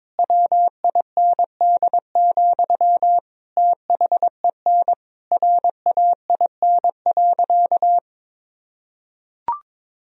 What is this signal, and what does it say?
Morse code, heard as a keyed tone, about 22 wpm, 700 Hz. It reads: WIND, THEN RAIN.
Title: Sentences from Top 1000 Words - Review Part 2 - 22wpm